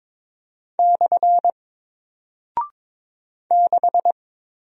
Morse code, heard as a keyed tone, about 21 wpm, 700 Hz. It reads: / 6
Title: Single Character (Letter:Number:Punctuation:Prosign) Alt 22wpm